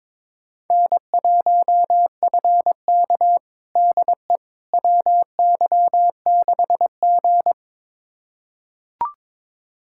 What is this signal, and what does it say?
Morse code, heard as a keyed tone, about 22 wpm, 700 Hz. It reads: N1FK DE WY6G
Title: USA Callsign DE Callsign - 4 Characters - 22wpm